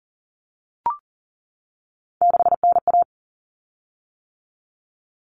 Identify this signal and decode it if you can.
Morse code, heard as a keyed tone, about 40 wpm, 700 Hz. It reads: 6NA
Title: Common Cut Numbers - 3-Digit 40wpm